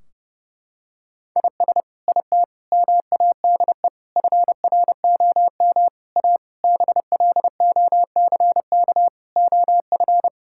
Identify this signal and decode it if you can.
Morse code, heard as a keyed tone, about 30 wpm, 700 Hz. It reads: IS IT MADE FROM A BLOCK OF
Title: Sentences from Top 1600 Words 30wpm